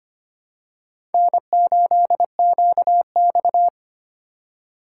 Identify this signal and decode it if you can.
Morse code, heard as a keyed tone, about 25 wpm, 700 Hz. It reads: N8QX
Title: USA Call Signs - 4 Characters - 3x Character Spacing - 25wpm